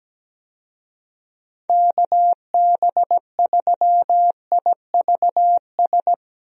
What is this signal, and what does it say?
Morse code, heard as a keyed tone, about 17 wpm, 700 Hz. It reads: KB3IVS
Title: USA Call Signs - 6 Characters - Repeated 2x - 17wpm